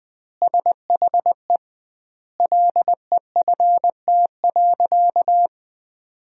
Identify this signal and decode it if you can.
Morse code, heard as a keyed tone, about 20 wpm, 700 Hz. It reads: SHE LEFT.
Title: 2x Word-Spacing - Sentences from Top 300 Words - 20wpm